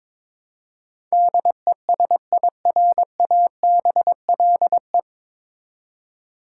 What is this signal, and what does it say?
Morse code, heard as a keyed tone, about 22 wpm, 700 Hz. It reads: DESIRABLE
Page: Morse Code Ninja practice 22wpm